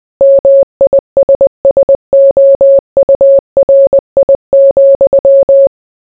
Morse code, heard as a keyed tone, about 20 wpm, 550 Hz. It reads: MISSOURI,